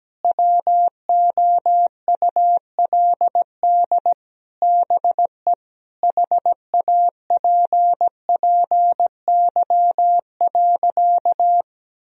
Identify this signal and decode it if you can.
Morse code, heard as a keyed tone, about 17 wpm, 700 Hz. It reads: WOULD BE HAPPY.
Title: Sentences from Top 1000 Words - Review Part 3 - 17wpm